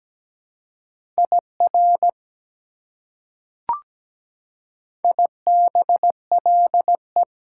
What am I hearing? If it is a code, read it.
Morse code, heard as a keyed tone, about 17 wpm, 700 Hz. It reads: IR IBLE
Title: Prefixes and Suffixes 17wpm